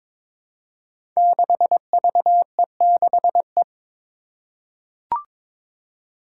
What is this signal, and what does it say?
Morse code, heard as a keyed tone, about 21 wpm, 700 Hz. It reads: 6VE6E
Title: Five Challenging Letter-Number Combinations 22wpm